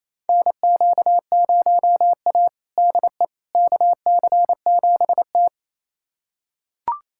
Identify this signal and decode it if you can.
Morse code, heard as a keyed tone, about 28 wpm, 700 Hz. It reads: NQ0A DE KC7T
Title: USA Callsign DE Callsign - 4 Characters - 28wpm